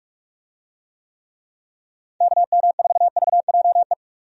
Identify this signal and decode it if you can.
Morse code, heard as a keyed tone, about 46 wpm, 700 Hz. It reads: KM4VJE
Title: USA Call Signs - 6 Characters - 45wpm